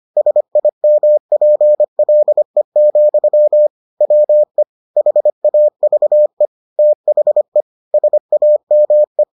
Morse code, heard as a keyed tone, about 25 wpm, 600 Hz. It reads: SIMPLE, WE HAVE THE SAME